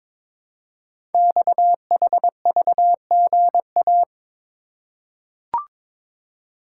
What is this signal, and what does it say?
Morse code, heard as a keyed tone, about 23 wpm, 700 Hz. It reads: XHVGA
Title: Code Groups - 5 Letters - 22wpm